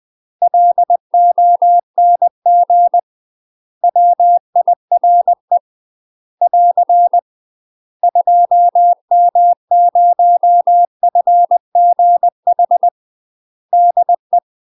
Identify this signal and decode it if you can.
Morse code, heard as a keyed tone, about 20 wpm, 700 Hz. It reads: LONG WIRE <AR> 2M0FGH DE
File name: UK MASTER QSOs TRAINING / 20WPM / UK MASTER QSOs 0003